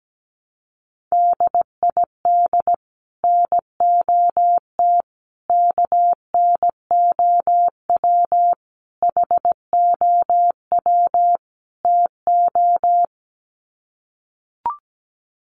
Morse code, heard as a keyed tone, about 17 wpm, 700 Hz. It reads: DID NOT KNOW HOW TO
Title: Sets of 5 Words 17wpm